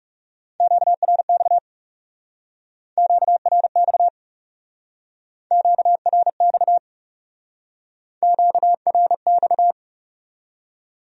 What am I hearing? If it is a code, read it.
Morse code, heard as a keyed tone, about 40 wpm, 700 Hz. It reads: QRX QRX QRX QRX